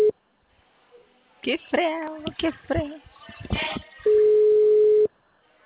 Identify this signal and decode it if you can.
Ringback tone (Europe, 425 Hz), with speech.
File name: out- -2006- - - .33